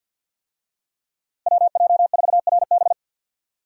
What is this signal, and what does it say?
Morse code, heard as a keyed tone, about 49 wpm, 700 Hz. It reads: WJ4RB
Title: USA Call Signs - 5 Characters - Repeated 3x - 50wpm